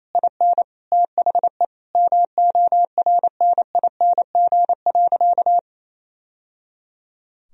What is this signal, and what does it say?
Morse code, heard as a keyed tone, about 28 wpm, 700 Hz. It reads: IN THE MORNING.
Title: Sentences from Top 1000 Words - Review Part 2 - 28wpm